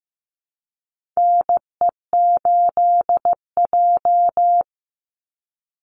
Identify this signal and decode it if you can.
Morse code, heard as a keyed tone, about 15 wpm, 700 Hz. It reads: NE8J
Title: Call Signs Repeated 3x - 15wpm